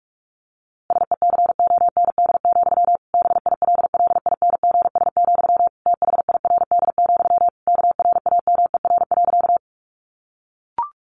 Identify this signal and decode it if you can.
Morse code, heard as a keyed tone, about 44 wpm, 700 Hz. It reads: SECOND, BILLINGS, THIRD, KRAMER.
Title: A Sound Of Thunder 45wpm